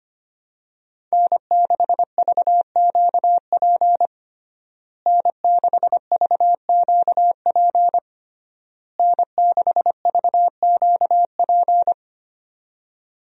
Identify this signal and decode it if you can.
Morse code, heard as a keyed tone, about 25 wpm, 700 Hz. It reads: N6VQP N6VQP N6VQP
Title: USA Call Signs - 5 Characters - Repeated 3x - 25wpm